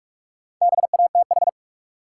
Morse code, heard as a keyed tone, about 46 wpm, 700 Hz. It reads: BATH
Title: Sentences from Top 1600 Words - New Word Drill Alt - 45wpm